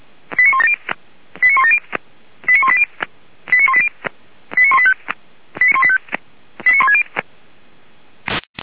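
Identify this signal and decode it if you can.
Five-tone selective call (ZVEI), heard as quick runs of five tones, tones of about 70 ms. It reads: 89170 89170 89180 89180 89185 89160
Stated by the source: ZVEI five-tone selective call 89132 received on an SDR, from the Feuerwehr Orlamünde audio server.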